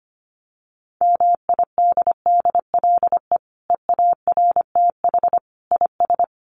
Morse code, heard as a keyed tone, about 25 wpm, 700 Hz. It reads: MIDDLE EARTH IS
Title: Sentences from Top 900 Words 25wpm